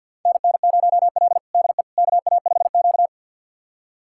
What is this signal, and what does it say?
Morse code, heard as a keyed tone, about 50 wpm, 700 Hz. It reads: NN0L DE KA5Q
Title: USA Callsign DE Callsign - 4 Characters - 50wpm